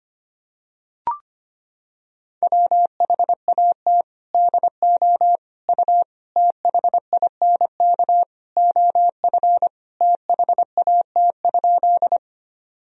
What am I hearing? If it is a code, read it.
Morse code, heard as a keyed tone, about 25 wpm, 700 Hz. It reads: WHAT DO U THINK OF THAT?
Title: Sentences from Top 200 Words and Abbreviations 25wpm